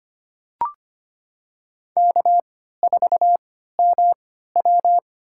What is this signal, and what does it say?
Morse code, heard as a keyed tone, about 25 wpm, spaced as for 14 wpm, 700 Hz. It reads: K4MW
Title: USA Call Signs - 4 Characters - 3x Character Spacing - 25wpm